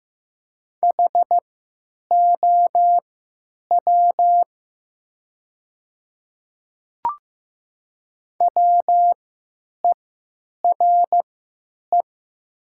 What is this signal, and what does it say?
Morse code, heard as a keyed tone, about 15 wpm, spaced as for 9 wpm, 700 Hz. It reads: HOW WERE
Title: 3x Character Spacing - Top 100 Words - 15wpm